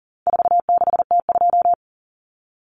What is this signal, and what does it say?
Morse code, heard as a keyed tone, about 40 wpm, 700 Hz. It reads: 46T2
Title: Common Cut Numbers - 4-Digit 40wpm